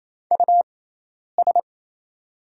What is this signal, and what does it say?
Morse code, heard as a keyed tone, about 27 wpm, spaced as for 10 wpm, 700 Hz. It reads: US
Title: 6x Character Spacing - Top 200 Words - 28wpm